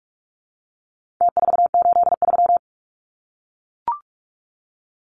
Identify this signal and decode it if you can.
Morse code, heard as a keyed tone, about 44 wpm, 700 Hz. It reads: T483
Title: Numbers - Contest Style - 4-Digit 45wpm